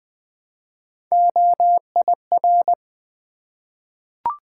Morse code, heard as a keyed tone, about 20 wpm, 700 Hz. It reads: OIR
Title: Three Letter-Number Combinations - 20wpm